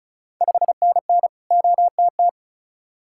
Morse code, heard as a keyed tone, about 36 wpm, 690 Hz. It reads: 5NN OTT
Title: Ontario QSO Party Exchange 35wpm